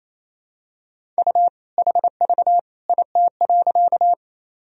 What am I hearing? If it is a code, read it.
Morse code, heard as a keyed tone, about 28 wpm, 700 Hz. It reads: U HV IT.
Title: Sentences from Top 100 Words and Abbreviations 28wpm